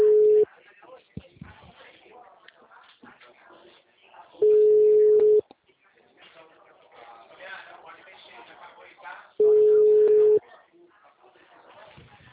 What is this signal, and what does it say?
Ringback tone (Europe, 425 Hz), with speech.